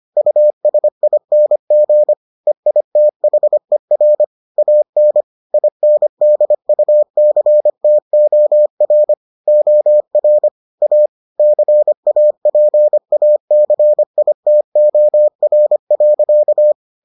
Morse code, heard as a keyed tone, about 25 wpm, 600 Hz. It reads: USING EITHER AN INDUCTOR OR A CAPACITOR.